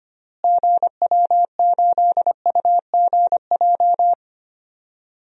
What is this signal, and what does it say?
Morse code, heard as a keyed tone, about 25 wpm, 700 Hz. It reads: GW8UGJ
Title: DXCC-100 Call Signs - 6 Characters - Repeated 3x - 25wpm